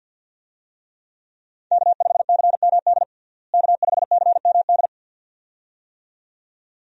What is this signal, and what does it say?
Morse code, heard as a keyed tone, about 50 wpm, 700 Hz. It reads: K5CMD K5CMD
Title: USA Call Signs - 5 Characters - Repeated 2x - 50wpm